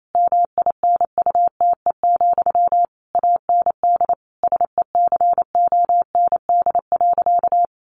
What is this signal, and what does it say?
Morse code, heard as a keyed tone, about 28 wpm, 700 Hz. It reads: MINUTE, AND SECOND.
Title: A Sound Of Thunder 28wpm